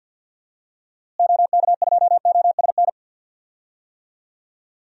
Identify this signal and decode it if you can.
Morse code, heard as a keyed tone, about 51 wpm, 700 Hz. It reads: OK2OSN
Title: DXCC-100 Call Signs - 6 Characters - Repeated 3x - 50wpm